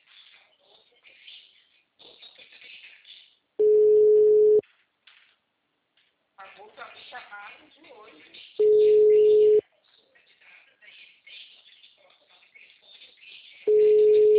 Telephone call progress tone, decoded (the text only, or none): ringback tone